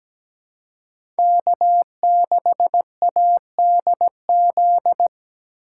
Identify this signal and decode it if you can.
Morse code, heard as a keyed tone, about 17 wpm, 700 Hz. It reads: K6ADZ
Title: USA Call Signs - 3x Character Spacing - 17wpm